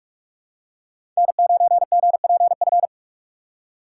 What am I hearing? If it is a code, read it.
Morse code, heard as a keyed tone, about 43 wpm, 700 Hz. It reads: N9GPF
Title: USA Call Signs - 5 Characters - 6x Character Spacing - 45wpm